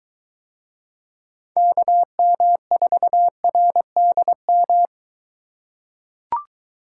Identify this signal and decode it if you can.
Morse code, heard as a keyed tone, about 23 wpm, 700 Hz. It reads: KM4RDM